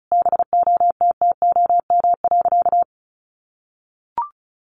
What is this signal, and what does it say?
Morse code, heard as a keyed tone, about 34 wpm, 700 Hz. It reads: BOTTOM.
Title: Easy Phrases 35wpm